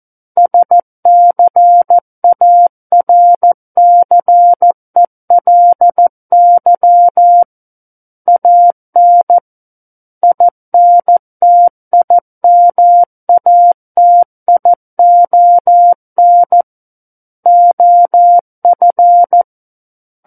Morse code, heard as a keyed tone, about 14 wpm, 700 Hz. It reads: SCARCELY AN INTIMATION OF
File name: Book One / 05-Chapter 5 - 14 WPM